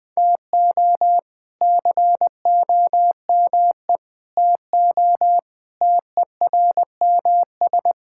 Morse code, heard as a keyed tone, about 20 wpm, 700 Hz. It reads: TO COME TO TERMS
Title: Sets of 8 Words - 20wpm